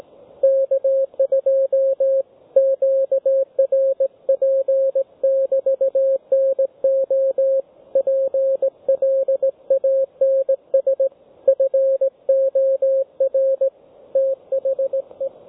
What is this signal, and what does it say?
Morse code, heard as a keyed tone, about 17 wpm, 540 Hz, with radio noise. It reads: K2 QRP<BT>NO PLANS FOR THE
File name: GW3UEP / G3VTT OneValveTX